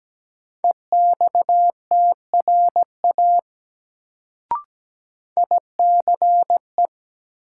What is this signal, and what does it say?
Morse code, heard as a keyed tone, about 17 wpm, 700 Hz. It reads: EXTRA ICE
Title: Sentences from Top 1000 Words - New Word Drill - 17wpm